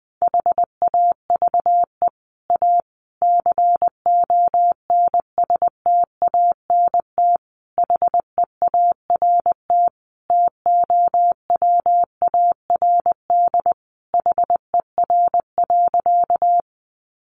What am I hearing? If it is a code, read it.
Morse code, heard as a keyed tone, about 20 wpm, 700 Hz. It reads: HAVE A CONSTANT HEART TOWARD HER.
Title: Sentences from Top 1500 Words 20wpm